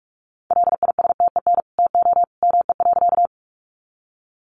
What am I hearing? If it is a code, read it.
Morse code, heard as a keyed tone, about 44 wpm, 700 Hz. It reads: LISTEN TO ME.
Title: Sentences from Top 400 Words 45wpm